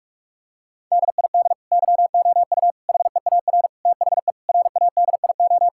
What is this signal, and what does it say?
Morse code, heard as a keyed tone, about 41 wpm, 700 Hz. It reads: DID YOU HEAR THE RADIO
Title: Sentences from Top 1000 Words - Review Part 5 - 45wpm